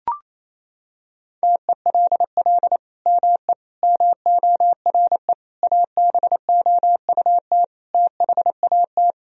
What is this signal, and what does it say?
Morse code, heard as a keyed tone, about 27 wpm, 700 Hz. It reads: TELL ME MORE ABOUT THAT
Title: Sentences from Top 1000 Words - Review Part 3 - 28wpm